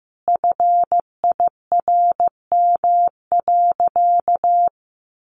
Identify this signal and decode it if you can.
Morse code, heard as a keyed tone, about 15 wpm, 700 Hz. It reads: FIRM.
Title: Sentences from Top 1400 Words 15wpm